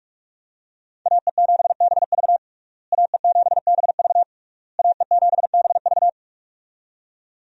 Morse code, heard as a keyed tone, about 47 wpm, 700 Hz. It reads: AE7BV AE7BV AE7BV
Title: USA Call Signs - 5 Characters - Repeated 3x - 45wpm